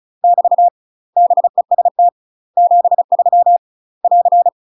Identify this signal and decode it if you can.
Morse code, heard as a keyed tone, about 36 wpm, 700 Hz. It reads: <BT> BEST 73 <AR>
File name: UK MASTER QSOs TRAINING / 35WPM / UK MASTER QSOs 0042